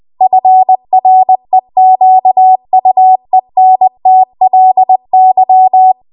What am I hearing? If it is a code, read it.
Morse code, heard as a keyed tone, about 20 wpm, 750 Hz. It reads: FREQUENTLY
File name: Archive / 20 WPM / 20WPM